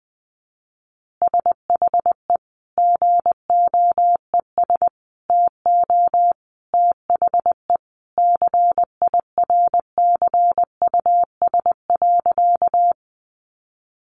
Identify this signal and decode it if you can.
Morse code, heard as a keyed tone, about 20 wpm, 700 Hz. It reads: SHE GOES TO THE CIRCUS.
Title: Easy Phrases 20wpm